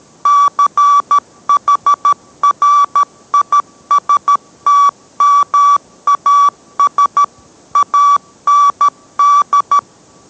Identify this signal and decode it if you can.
Morse code, heard as a keyed tone, about 13 wpm, 1150 Hz. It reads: CHRISTMAS AND